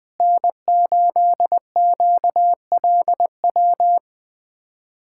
Morse code, heard as a keyed tone, about 21 wpm, 700 Hz. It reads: N8QLW
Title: USA Callsign DE Callsign - 5 Characters - 20wpm